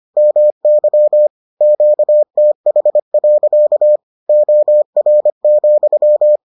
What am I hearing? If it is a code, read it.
Morse code, heard as a keyed tone, about 25 wpm, 600 Hz. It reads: MY QTH. OR,